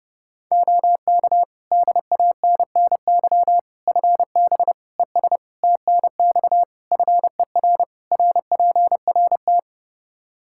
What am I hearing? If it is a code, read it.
Morse code, heard as a keyed tone, about 30 wpm, 700 Hz. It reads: OK DANNY FB ES TNX FER RPRT
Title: Practice QSO - Exchange 2 - Info - 30wpm